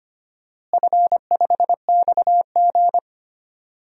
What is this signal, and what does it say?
Morse code, heard as a keyed tone, about 25 wpm, 700 Hz. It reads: F5XG
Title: DXCC-100 Call Signs - 4 Characters - Repeated 3x - 25wpm